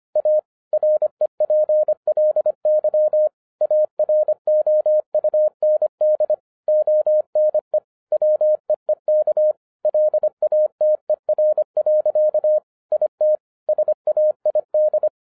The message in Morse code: A REPLY AROUND ONE WEEK LATER. IT SAID

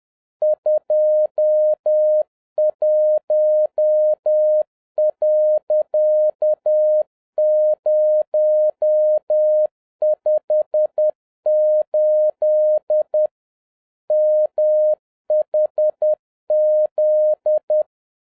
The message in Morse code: 21.058 MHZ

